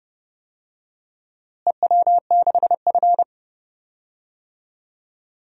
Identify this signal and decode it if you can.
Morse code, heard as a keyed tone, about 31 wpm, 700 Hz. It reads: EW6F
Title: DXCC-100 Call Signs - 4 Characters - Repeated 3x - 30wpm